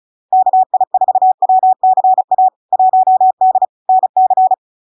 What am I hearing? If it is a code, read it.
Morse code, heard as a keyed tone, about 36 wpm, 750 Hz. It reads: KI4WCA 1D NC
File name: LARK-FD-2022-Log-in-CW-35wpm-750-Hz